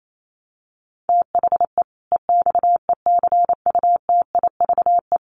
Morse code, heard as a keyed tone, about 28 wpm, 700 Hz. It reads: THE EXECUTIVE
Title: Sets of 5 Words Encore 28wpm